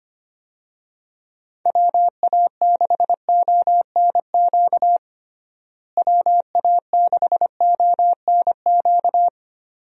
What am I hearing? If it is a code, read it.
Morse code, heard as a keyed tone, about 25 wpm, 700 Hz. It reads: WA6ONQ WA6ONQ
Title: USA Call Signs - 6 Characters - Repeated 2x - 25wpm